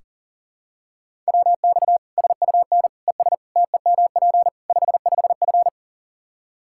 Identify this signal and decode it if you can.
Morse code, heard as a keyed tone, about 40 wpm, 700 Hz. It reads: WX SUN ES TEMP 55F
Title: Practice QSO - Exchange 2 - Info - 40wpm